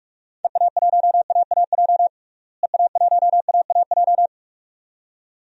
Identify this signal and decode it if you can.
Morse code, heard as a keyed tone, about 41 wpm, 700 Hz. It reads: EA1AAJ EA1AAJ
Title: DXCC-100 Call Signs - 6 Characters - Repeated 2x - 45wpm